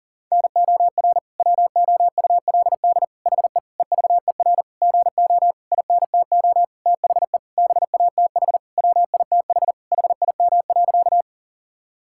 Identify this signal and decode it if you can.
Morse code, heard as a keyed tone, about 37 wpm, 700 Hz. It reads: NOR WOULD HE EVER GO INTO THE BATH WITH HIM.
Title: Sentences from Top 1600 Words 40wpm